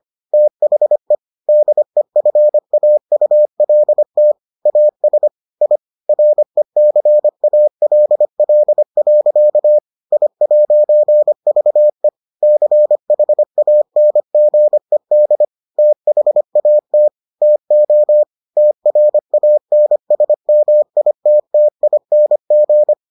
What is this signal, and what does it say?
Morse code, heard as a keyed tone, about 25 wpm, 600 Hz. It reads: THE DEFAULT AS I RECALL. I'VE CHANGED THAT TO TRANSMITTING